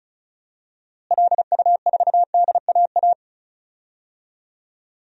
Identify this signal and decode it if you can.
Morse code, heard as a keyed tone, about 34 wpm, 700 Hz. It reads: LU4DAA
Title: DXCC-100 Call Signs - 6 Characters - Repeated 3x - 35wpm